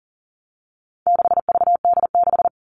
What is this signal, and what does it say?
Morse code, heard as a keyed tone, about 40 wpm, 700 Hz. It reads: 6VD6
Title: All Cut Numbers - 4-Digit 40wpm